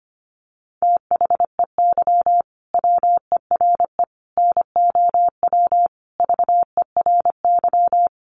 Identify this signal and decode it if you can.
Morse code, heard as a keyed tone, about 25 wpm, 700 Hz. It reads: THEY WERE NOW VERY
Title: Sentences from Top 1400 Words 25wpm